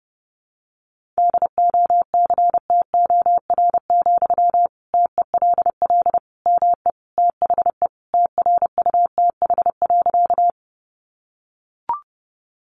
Morse code, heard as a keyed tone, about 30 wpm, 700 Hz. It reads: DOCTOR, TELL ME THE TRUTH.
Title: Sentences from Top 1500 Words 30wpm